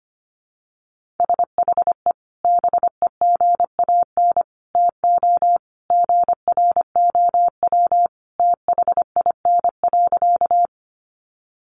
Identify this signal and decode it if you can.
Morse code, heard as a keyed tone, about 25 wpm, 700 Hz. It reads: SHE BEGAN TO GROW THIN.
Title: Sentences from Top 1200 Words 25wpm